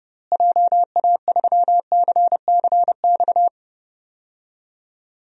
Morse code, heard as a keyed tone, about 30 wpm, 700 Hz. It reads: JA3CCX
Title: DXCC-100 Call Signs - 6 Characters - Repeated 3x - 30wpm